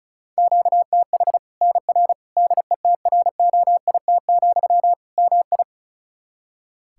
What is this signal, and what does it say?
Morse code, heard as a keyed tone, about 34 wpm, 700 Hz. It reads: QTH NR DETROIT, MI
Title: Practice QSO - Exchange 1 - Beginning - 35wpm